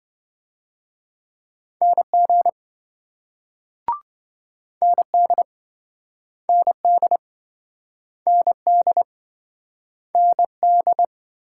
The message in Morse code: NG ND ND ND ND

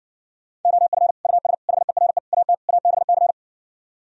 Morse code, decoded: OF US HERE AT ABC

